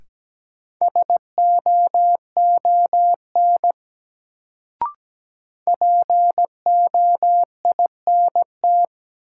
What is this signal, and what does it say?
Morse code, heard as a keyed tone, about 17 wpm, 700 Hz. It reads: SOON POINT
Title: Sets of 3 Words - New Word Drill - 17wpm